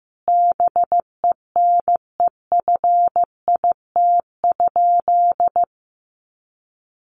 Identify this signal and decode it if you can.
Morse code, heard as a keyed tone, about 15 wpm, 700 Hz. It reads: BENEFIT?
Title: Sentences from Top 1500 Words 15wpm